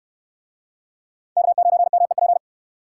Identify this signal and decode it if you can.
Morse code, heard as a keyed tone, about 69 wpm, 700 Hz. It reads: K9MEP